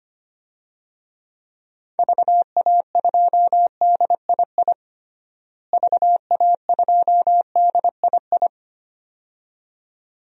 Morse code, heard as a keyed tone, about 25 wpm, 700 Hz. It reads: VA2DII VA2DII
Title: DXCC-100 Call Signs - 6 Characters - Repeated 2x - 25wpm